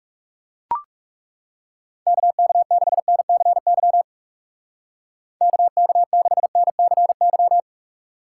Morse code, heard as a keyed tone, about 39 wpm, 700 Hz. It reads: KK6NCY KK6NCY